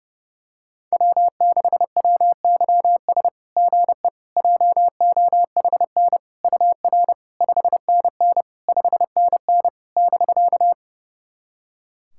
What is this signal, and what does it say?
Morse code, heard as a keyed tone, about 28 wpm, 690 Hz. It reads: W6WYS GE JOHN UR 5NN 5NN <BK>
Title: SOTA QSO Practice 30wpm